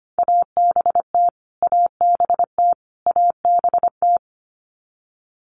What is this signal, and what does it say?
Morse code, heard as a keyed tone, about 25 wpm, 700 Hz. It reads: ABT ABT ABT
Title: Abbreviations Repeated 3x - 25wpm